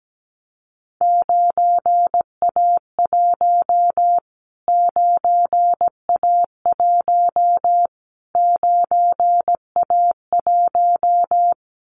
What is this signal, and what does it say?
Morse code, heard as a keyed tone, about 17 wpm, 700 Hz. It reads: 9A1 9A1 9A1
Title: Callsign Prefixes for Top 25 DXCC - Repeated 3x - 17wpm